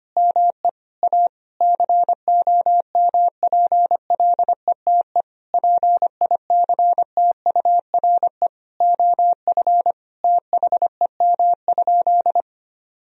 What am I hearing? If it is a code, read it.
Morse code, heard as a keyed tone, about 25 wpm, 700 Hz. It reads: ME A COMPLETE PICTURE OF THEM?
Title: Sentences from Top 1000 Words - Review Part 2 - 25wpm